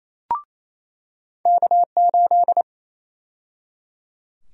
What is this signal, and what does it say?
Morse code, heard as a keyed tone, about 28 wpm, 700 Hz. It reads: K8
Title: USA Call Sign Prefixes - 28wpm